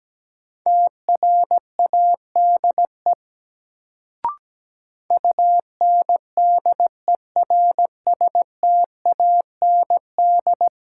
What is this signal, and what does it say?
Morse code, heard as a keyed tone, about 17 wpm, 700 Hz. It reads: TRADE UNDERSTAND
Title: Sentences from Top 600 Words - New Word Drill - 17wpm